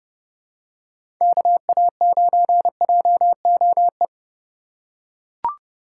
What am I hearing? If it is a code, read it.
Morse code, heard as a keyed tone, about 30 wpm, 700 Hz. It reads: KA9JOE
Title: USA Call Signs - 8x Character Spacing - 30wpm